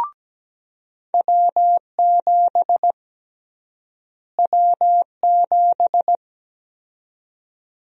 Morse code, heard as a keyed tone, about 17 wpm, 700 Hz. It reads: W7 W7
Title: USA Call Sign Prefixes - Repeated 2x - 17wpm